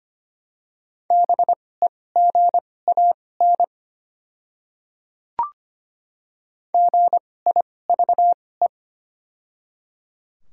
Morse code, heard as a keyed tone, about 25 wpm, spaced as for 18 wpm, 700 Hz. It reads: BEGAN GIVE
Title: 2x Character Spacing - Top 300 Words - 25wpm